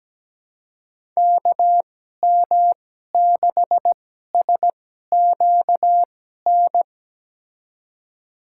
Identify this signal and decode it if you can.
Morse code, heard as a keyed tone, about 17 wpm, spaced as for 12 wpm, 700 Hz. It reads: KM6SQN
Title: USA Call Signs - 6 Characters - 2x Character Spacing - 17wpm